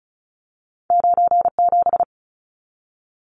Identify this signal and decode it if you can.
Morse code, heard as a keyed tone, about 36 wpm, 700 Hz. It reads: 97
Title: Numbers - 2-Digit 35wpm